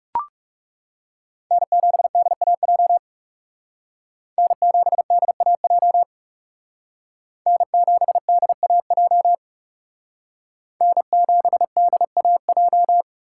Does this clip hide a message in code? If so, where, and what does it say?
Yes: Morse code from 1.5 to 13.0 s, heard as a keyed tone, about 39 wpm, 700 Hz. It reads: N7DAJ N7DAJ N7DAJ N7DAJ